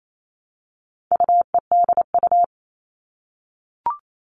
Morse code, heard as a keyed tone, about 28 wpm, 700 Hz. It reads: UEDU
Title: All Cut Numbers - 4-Digit 28wpm